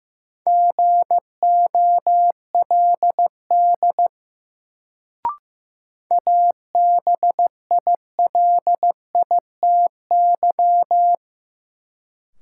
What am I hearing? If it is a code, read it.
Morse code, heard as a keyed tone, about 15 wpm, 700 Hz. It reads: GOLD ABILITY
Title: Sentences from Top 1000 Words - Review Part 3 - New Word Drill - 15wpm